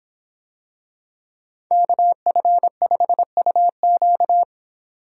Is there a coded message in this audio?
Yes: Morse code from 1.7 to 4.4 s, heard as a keyed tone, about 26 wpm, 700 Hz. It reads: KF5UQ